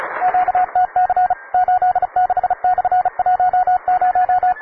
Morse code, heard as a keyed tone, about 36 wpm, 730 Hz, with radio noise. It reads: QTC 86/10
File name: wae / cw advanced / rk9jwv cw